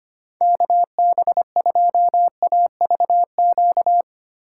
Morse code, heard as a keyed tone, about 25 wpm, 700 Hz. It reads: KB2AVQ
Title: USA Call Signs - 6 Characters - 4x Character Spacing - 25wpm